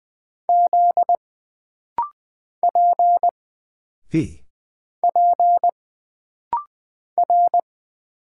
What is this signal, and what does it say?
Morse code, heard as a keyed tone, about 20 wpm, 700 Hz. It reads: Z P P R
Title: Single Character (Letter:Number:Punctuation:Prosign) - Rapid-Fire - 20wpm